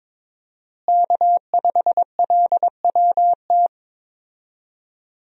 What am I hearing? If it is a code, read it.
Morse code, heard as a keyed tone, about 22 wpm, 700 Hz. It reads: K5LWT
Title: USA Call Signs - 5 Characters - Repeated 3x - 22wpm